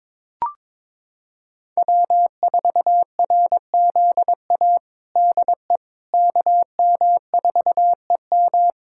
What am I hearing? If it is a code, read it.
Morse code, heard as a keyed tone, about 22 wpm, 700 Hz. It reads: W4RZA DE KM4EM
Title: Practice QSO - Exchange 4 - Ending - 22wpm